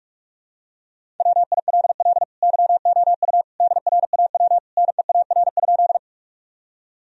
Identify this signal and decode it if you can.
Morse code, heard as a keyed tone, about 46 wpm, 700 Hz. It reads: WILL YOU DRAW NEAR?
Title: Sentences from Top 1000 Words - Review Part 1 - 45wpm